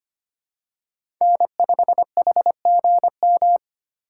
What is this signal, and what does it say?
Morse code, heard as a keyed tone, about 25 wpm, 700 Hz. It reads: N5HGM
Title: USA Call Signs - 4x Character Spacing - 25wpm